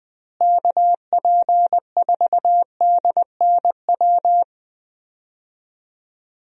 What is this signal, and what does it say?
Morse code, heard as a keyed tone, about 20 wpm, 700 Hz. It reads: KP4DNW
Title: USA Call Signs - 6 Characters - Repeated 3x - 20wpm